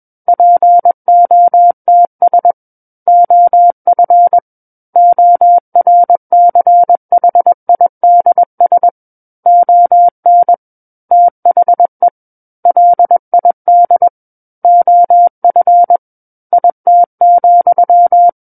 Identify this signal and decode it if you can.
Morse code, heard as a keyed tone, about 21 wpm, 700 Hz. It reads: POTS OF ORCHIDS ON THE LID OF IT,
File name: Book One / 12-Chapter 12 - 21 WPM